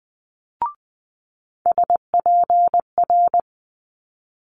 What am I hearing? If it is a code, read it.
Morse code, heard as a keyed tone, about 20 wpm, 700 Hz. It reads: SPR
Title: Multi-letter Phonemes Quick Fire 20wpm